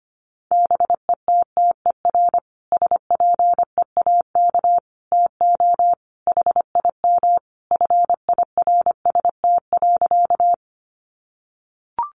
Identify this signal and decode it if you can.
Morse code, heard as a keyed tone, about 25 wpm, 700 Hz. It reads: BETTER SPEAK TO HIM FIRST.
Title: Sentences from Top 1100 Words 25wpm